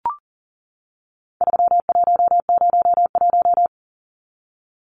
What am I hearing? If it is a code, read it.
Morse code, heard as a keyed tone, about 40 wpm, 700 Hz. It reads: 3101
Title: Numbers - 4-Digit 40wpm